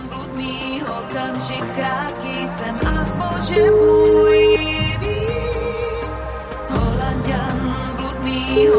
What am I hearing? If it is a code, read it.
Ringback tone (Europe, 425 Hz), with noise.